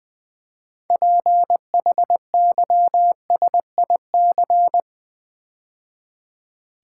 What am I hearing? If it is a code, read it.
Morse code, heard as a keyed tone, about 20 wpm, 700 Hz. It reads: PHYSIC
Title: Build A Word - 3 Letters At A Time - 9-Letter Words - 20wpm